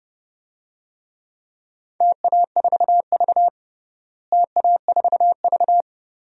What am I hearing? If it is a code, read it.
Morse code, heard as a keyed tone, about 29 wpm, 700 Hz. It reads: TA4V TA4V
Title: DXCC-100 Call Signs - 4 Characters - Repeated 3x - 30wpm